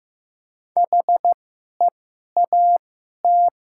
Morse code, heard as a keyed tone, about 15 wpm, spaced as for 11 wpm, 700 Hz. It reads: HEAT
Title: 2x Character Spacing - Top 500 Words - 15wpm